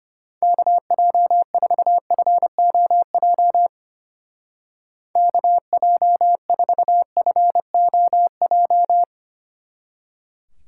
Morse code, heard as a keyed tone, about 29 wpm, 700 Hz. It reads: KJ4FOJ KJ4FOJ
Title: Speed Racing - USA 2x3 Call Signs 25wpm